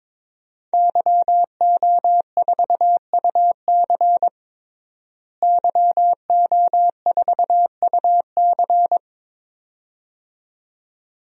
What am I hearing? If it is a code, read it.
Morse code, heard as a keyed tone, about 22 wpm, 700 Hz. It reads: YO4UC YO4UC
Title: DXCC-100 Call Signs - 5 Characters - Repeated 2x - 22wpm